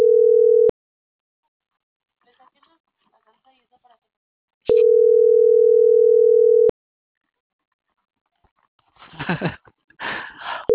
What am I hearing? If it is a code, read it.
Ringback tone (North America), 440 + 480 Hz.